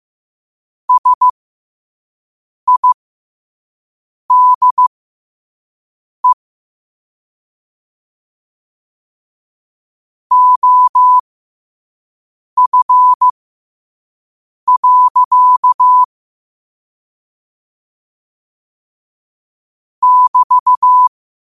Morse code, heard as a keyed tone, about 15 wpm, spaced as for 5 wpm, 1000 Hz. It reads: SIDE OF. <BT>
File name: qotdfiles / 5wpm